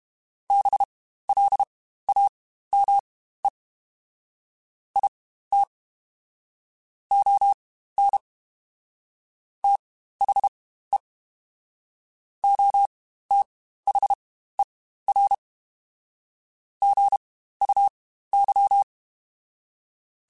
Morse code, heard as a keyed tone, about 31 wpm, spaced as for 15 wpm, 800 Hz. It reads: BLAME IT ON THE OTHER GUY
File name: Spaced / Chuckles Spaced-mp3s / HumanErr sp @30